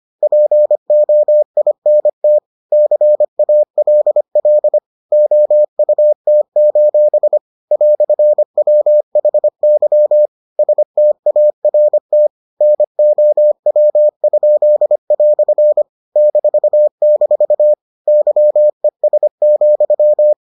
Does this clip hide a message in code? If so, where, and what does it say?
Yes: Morse code from 0.2 to 20.3 s, heard as a keyed tone, about 25 wpm, 600 Hz. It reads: POINT CALL OUT: "WHY START NOW?" -- YES,